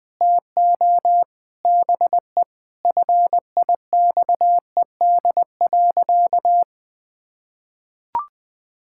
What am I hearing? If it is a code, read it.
Morse code, heard as a keyed tone, about 20 wpm, 700 Hz. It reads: TO BE FIXED.
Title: Sentences with 5-Word Phrases - 20wpm